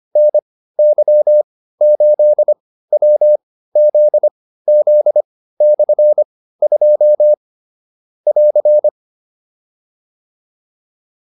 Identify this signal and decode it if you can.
Morse code, heard as a keyed tone, about 25 wpm, spaced as for 15 wpm, 600 Hz. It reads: NY8WZZ/2 <AR>
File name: QSO301 15